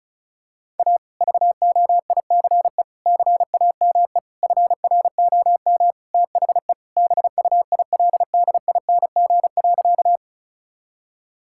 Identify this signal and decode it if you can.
Morse code, heard as a keyed tone, about 36 wpm, 700 Hz. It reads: A VOICE CAME FROM THE BUILDING.
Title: Sentences from Top 1000 Words - Review Part 5 - 35wpm